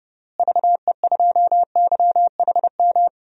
Morse code, heard as a keyed tone, about 31 wpm, 700 Hz. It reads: VE2YHM
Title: DXCC-100 Call Signs - 6 Characters - Repeated 2x - 30wpm